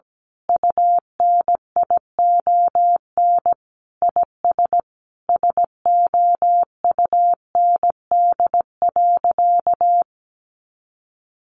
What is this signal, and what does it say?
Morse code, heard as a keyed tone, about 17 wpm, 700 Hz. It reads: UNION IS SOUND.
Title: Sentences from Top 1400 Words 17wpm